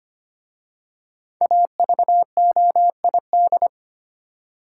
Morse code, heard as a keyed tone, about 25 wpm, 700 Hz. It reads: AVOID
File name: 5 Letter Words - First 3 Letters Given - 25wpm